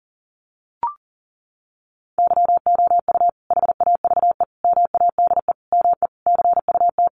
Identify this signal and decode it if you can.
Morse code, heard as a keyed tone, about 40 wpm, 700 Hz. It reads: YOU HAVE MADE ME CUT
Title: Sentences from Top 1300 Words 40wpm